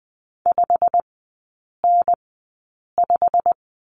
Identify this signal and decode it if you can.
Morse code, heard as a keyed tone, about 20 wpm, spaced as for 8 wpm, 700 Hz. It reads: 5N5
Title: 107 Lesson 18e - Letter m - Three Characters 20wpm